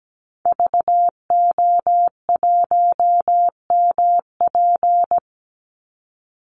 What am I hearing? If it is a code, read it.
Morse code, heard as a keyed tone, about 17 wpm, 700 Hz. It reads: VO1MP
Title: Call Signs Repeated 2x - 17wpm